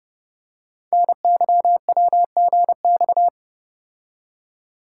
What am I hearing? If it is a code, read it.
Morse code, heard as a keyed tone, about 28 wpm, 700 Hz. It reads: NYWGX
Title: Code Groups - 5 Letter:Numbers - 30wpm